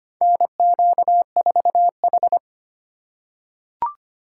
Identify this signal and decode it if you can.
Morse code, heard as a keyed tone, about 25 wpm, 700 Hz. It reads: NQ4H
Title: USA Call Signs - 4 Characters - 4x Character Spacing - 25wpm